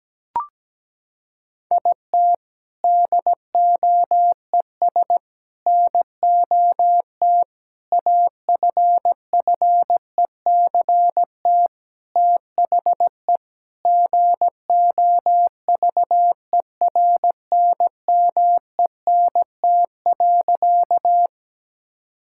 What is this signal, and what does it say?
Morse code, heard as a keyed tone, about 17 wpm, 700 Hz. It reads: IT DOES NOT AFFECT THE GOVERNMENT.
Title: Sentences from Top 1000 Words - Review Part 6 - 17wpm